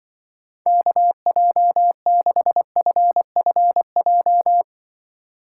Morse code, heard as a keyed tone, about 25 wpm, 700 Hz. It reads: KJ6FFJ
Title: Speed Racing - USA 2x3 Call Signs 20wpm